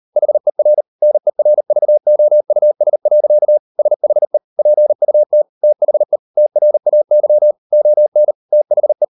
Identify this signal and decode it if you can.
Morse code, heard as a keyed tone, about 38 wpm, 600 Hz. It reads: HER NERVOUS. SHE PUT THE TRAY ON THE